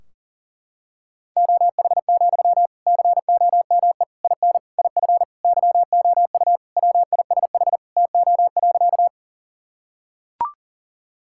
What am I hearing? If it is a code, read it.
Morse code, heard as a keyed tone, about 40 wpm, 700 Hz. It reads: OH, COME IN IF YOU WISH TO.
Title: Sentences from Top 1600 Words 40wpm